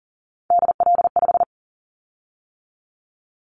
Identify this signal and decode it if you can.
Morse code, heard as a keyed tone, about 42 wpm, 700 Hz. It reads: DL5
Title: Callsign Prefixes for Top 50 DXCC - 40wpm